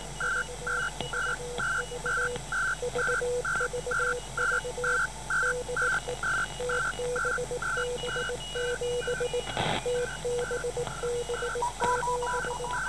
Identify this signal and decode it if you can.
Morse code, heard as a keyed tone, about 19 wpm, 490 Hz, with noise. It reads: VVV VVV DE TBB7 TBB7